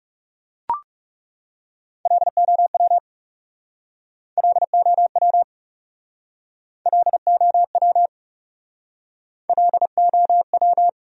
Morse code, heard as a keyed tone, about 37 wpm, 700 Hz. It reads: LOW LOW LOW LOW
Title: Speed-Racing Top 500 Words 30wpm